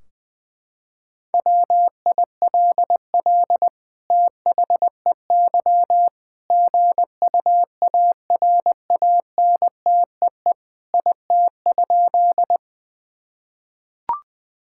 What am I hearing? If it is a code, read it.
Morse code, heard as a keyed tone, about 21 wpm, 700 Hz. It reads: WILL THEY GUARANTEE IT?
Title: Sentences from Top 1600 Words 20wpm